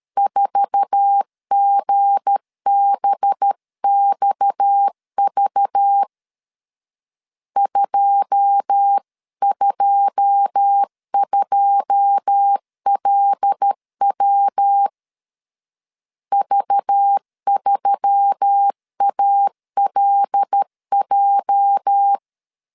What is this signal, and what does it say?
Morse code, heard as a keyed tone, about 13 wpm, 800 Hz. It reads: 4GBXV 222LW V3ALJ
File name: CW / DET / 1 9 1500C S B 12wpm 3819